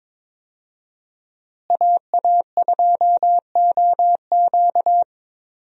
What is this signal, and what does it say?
Morse code, heard as a keyed tone, about 22 wpm, 700 Hz. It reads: AA2OQ
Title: USA Call Signs - 5 Characters - Repeated 2x - 22wpm